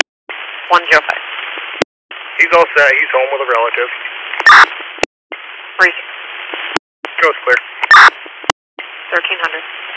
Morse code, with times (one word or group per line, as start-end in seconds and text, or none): none